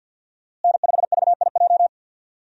Morse code, heard as a keyed tone, about 52 wpm, 700 Hz. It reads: N5VIJ
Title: USA Call Signs - 5 Characters - Repeated 2x - 50wpm